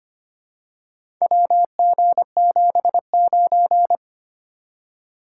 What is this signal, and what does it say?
Morse code, heard as a keyed tone, about 25 wpm, 700 Hz. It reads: WG79
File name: practice / 25wpm / Four Letter-Number Combinations - 25wpm